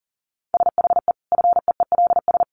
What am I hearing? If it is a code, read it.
Morse code, heard as a keyed tone, about 40 wpm, 700 Hz. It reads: SHE FEELS
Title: Easy Phrases 40wpm